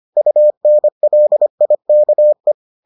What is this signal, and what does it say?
Morse code, heard as a keyed tone, about 25 wpm, 600 Hz. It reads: UNLIKE